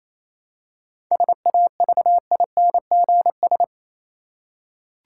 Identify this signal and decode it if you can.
Morse code, heard as a keyed tone, about 27 wpm, 700 Hz. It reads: SAVINGS
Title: Seven Letter Words - 28wpm